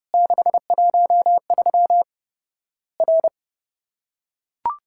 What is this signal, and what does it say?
Morse code, heard as a keyed tone, about 30 wpm, 700 Hz. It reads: 613 R
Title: Worked All Europe DX Contest - QTC Exchange - 30wpm